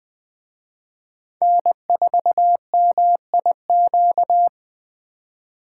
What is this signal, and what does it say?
Morse code, heard as a keyed tone, about 19 wpm, 700 Hz. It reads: N4MIQ